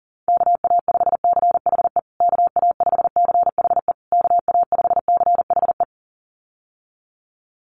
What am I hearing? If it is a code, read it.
Morse code, heard as a keyed tone, about 40 wpm, 700 Hz. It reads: KA5CHE KA5CHE KA5CHE
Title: USA Call Signs Repeated 3x 40wpm